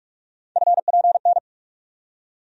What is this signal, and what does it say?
Morse code, heard as a keyed tone, about 43 wpm, 700 Hz. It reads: FPN
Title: USA Call Sign Postfixes - 3 Letters 45wpm